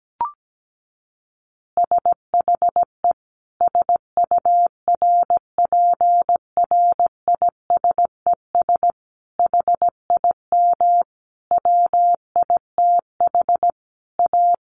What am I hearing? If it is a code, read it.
Morse code, heard as a keyed tone, about 17 wpm, 700 Hz. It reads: SHE SURPRISES HIM WITH A
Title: Additional Easy Phrases 17wpm